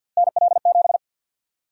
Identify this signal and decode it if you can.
Morse code, heard as a keyed tone, about 51 wpm, 700 Hz. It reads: NL7
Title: USA Call Sign Prefixes - Repeated 2x - 50wpm